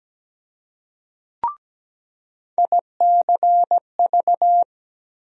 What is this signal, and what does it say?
Morse code, heard as a keyed tone, about 17 wpm, 700 Hz. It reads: ICV
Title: USA Call Sign Postfixes - 3 Letters 17wpm